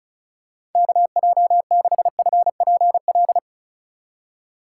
Morse code, heard as a keyed tone, about 36 wpm, 700 Hz. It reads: KJ6FPL
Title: USA Call Signs - 3x Character Spacing - 35wpm